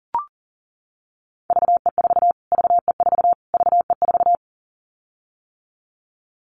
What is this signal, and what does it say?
Morse code, heard as a keyed tone, about 40 wpm, 700 Hz. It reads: VE4 VE4 VE4
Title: Callsign Prefixes for Top 25 DXCC - Repeated 3x - 40wpm